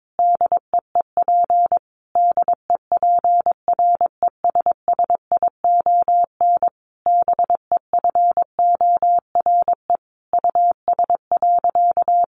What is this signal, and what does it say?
Morse code, heard as a keyed tone, about 22 wpm, 700 Hz. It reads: DEEP DEPRESSION BEFORE US.
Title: Sentences from Top 1300 Words 22wpm